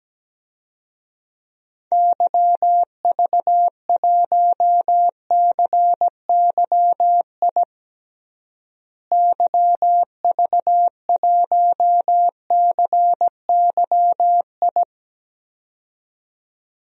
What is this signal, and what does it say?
Morse code, heard as a keyed tone, about 17 wpm, 700 Hz. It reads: YV1CYI YV1CYI
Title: DXCC-100 Call Signs - 6 Characters - Repeated 2x - 17wpm